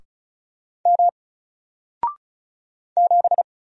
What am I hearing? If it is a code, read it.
Morse code, heard as a keyed tone, about 36 wpm, 700 Hz. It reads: M 7
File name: Single-Letter-Number-35wpm